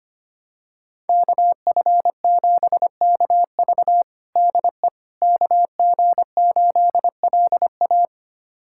Morse code, heard as a keyed tone, about 25 wpm, 700 Hz. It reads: KF7KV DE KG8LA
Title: USA Callsign DE Callsign - 5 Characters - 25wpm